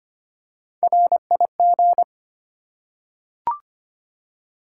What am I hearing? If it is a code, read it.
Morse code, heard as a keyed tone, about 25 wpm, 700 Hz. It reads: RIG
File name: Top 20 QSO Elements - 3x Character Spacing - 25wpm